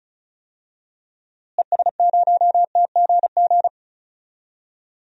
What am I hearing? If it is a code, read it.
Morse code, heard as a keyed tone, about 36 wpm, 700 Hz. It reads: ES0TGG
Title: DXCC-100 Call Signs - 6 Characters - Repeated 3x - 35wpm